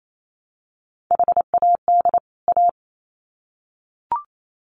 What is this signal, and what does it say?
Morse code, heard as a keyed tone, about 28 wpm, 700 Hz. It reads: HAD A
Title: Sets of 2 Words 28wpm